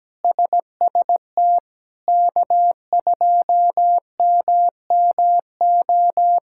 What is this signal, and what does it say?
Morse code, heard as a keyed tone, about 17 wpm, 700 Hz. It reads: SST K2MMO
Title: K1USN SST QSO Practice - 17wpm